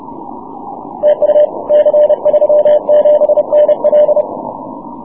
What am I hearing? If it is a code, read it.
Morse code, heard as a keyed tone, about 30 wpm, 600 Hz. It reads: TU C37NL
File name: wp / c37nl 6m cw